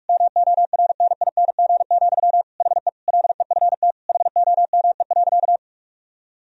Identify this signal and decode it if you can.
Morse code, heard as a keyed tone, about 41 wpm, 700 Hz. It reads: MORNING, HE LEFT HOME.
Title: Sentences from Top 1000 Words - Review Part 4 - 45wpm